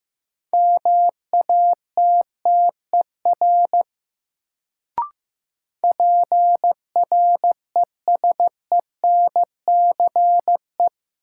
Morse code, heard as a keyed tone, about 15 wpm, 700 Hz. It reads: MATTER PRESENCE
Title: Sets of 3 Words - New Word Drill - 15wpm